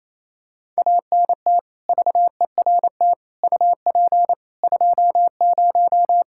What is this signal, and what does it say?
Morse code, heard as a keyed tone, about 28 wpm, 700 Hz. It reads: ANT VERT UP 20
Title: Practice QSO - Exchange 2 - Info - 28wpm